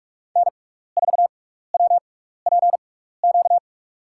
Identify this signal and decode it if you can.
Morse code, heard as a keyed tone, about 46 wpm, spaced as for 16 wpm, 700 Hz. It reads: N4WPQ
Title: USA Call Signs - 5 Characters - 6x Character Spacing - 45wpm